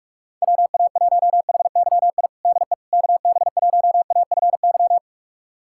Morse code, heard as a keyed tone, about 46 wpm, 700 Hz. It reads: WA1HYI DE KB1AFY
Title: USA Callsign DE Callsign - 6 Characters - 45wpm